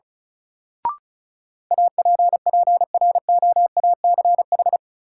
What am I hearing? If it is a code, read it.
Morse code, heard as a keyed tone, about 34 wpm, 700 Hz. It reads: APPROACH
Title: Sentences from Top 1400 Words - New Word Drill Alt - 35wpm